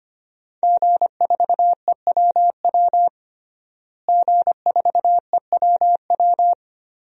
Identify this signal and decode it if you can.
Morse code, heard as a keyed tone, about 25 wpm, 700 Hz. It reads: G4EWW G4EWW
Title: DXCC-100 Call Signs - 5 Characters - Repeated 2x - 25wpm